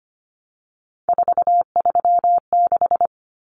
Morse code, heard as a keyed tone, about 25 wpm, 700 Hz. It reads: 436
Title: Numbers - 3-Digit 25wpm